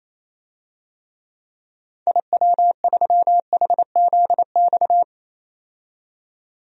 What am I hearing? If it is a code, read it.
Morse code, heard as a keyed tone, about 28 wpm, 700 Hz. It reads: IW3HZX
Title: DXCC-100 Call Signs - 6 Characters - 28wpm